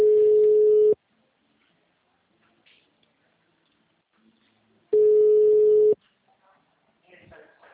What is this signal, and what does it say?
Ringback tone (Europe, 425 Hz), with speech.